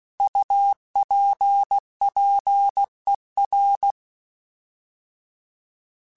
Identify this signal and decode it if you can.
Morse code, heard as a keyed tone, about 16 wpm, 800 Hz. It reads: UPPER
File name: Alphabetical / Alphabetical mp3s / Us @15